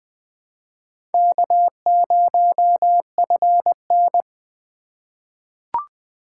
Morse code, heard as a keyed tone, about 20 wpm, 700 Hz. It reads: K0FN
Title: USA Call Signs - 4 Characters - 4x Character Spacing - 20wpm